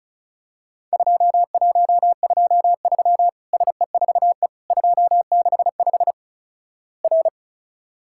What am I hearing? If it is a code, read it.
Morse code, heard as a keyed tone, about 36 wpm, 700 Hz. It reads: 2123 SE4E 265 R
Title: Worked All Europe DX Contest - QTC Exchange - 35wpm